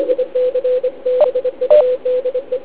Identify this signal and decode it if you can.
Morse code, heard as a keyed tone, about 24 wpm, 500 Hz. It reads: IC BUDE